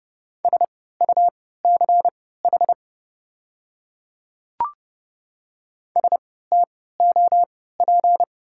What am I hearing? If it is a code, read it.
Morse code, heard as a keyed tone, about 30 wpm, spaced as for 17 wpm, 700 Hz. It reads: SUCH STOP
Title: 3x Character Spacing - Top 300 Words - 30wpm